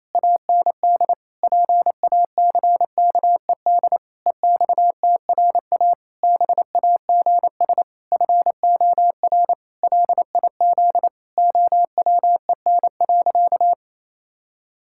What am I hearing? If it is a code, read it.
Morse code, heard as a keyed tone, about 28 wpm, 700 Hz. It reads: AND PACKED EXTRA BAGS FOR LIZ OWEN.
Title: Pangram Challenge v2 - 28wpm